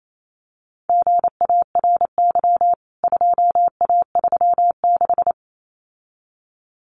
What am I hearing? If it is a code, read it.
Morse code, heard as a keyed tone, about 28 wpm, 700 Hz. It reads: GARY 2A36
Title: CWT Exchange 28wpm